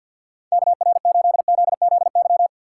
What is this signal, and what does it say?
Morse code, heard as a keyed tone, about 50 wpm, 700 Hz. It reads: KR8ZZY
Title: USA Call Signs - 6 Characters - 4x Character Spacing - 50wpm